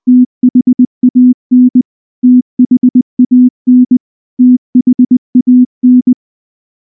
Morse code, heard as a keyed tone, about 19 wpm, 250 Hz. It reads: THAN THAN THAN
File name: morse / morse-training-20WPM-1